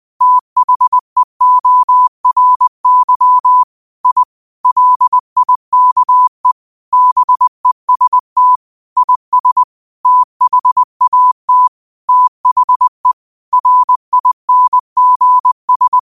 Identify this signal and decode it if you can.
Morse code, heard as a keyed tone, about 20 wpm, 1000 Hz. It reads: THEORY I LIKE BEST IS THAT THE RINGS